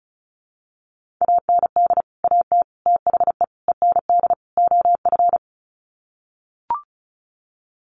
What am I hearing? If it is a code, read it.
Morse code, heard as a keyed tone, about 34 wpm, 700 Hz. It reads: AND AT THE END OF
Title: Sets of 5 Words 35wpm